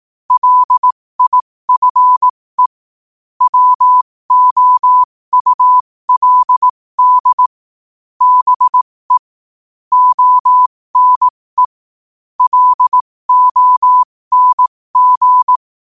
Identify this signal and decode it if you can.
Morse code, heard as a keyed tone, about 18 wpm, spaced as for 15 wpm, 1000 Hz. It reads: LIFE WOULD BE ONE LONG
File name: qotdfiles / 15wpm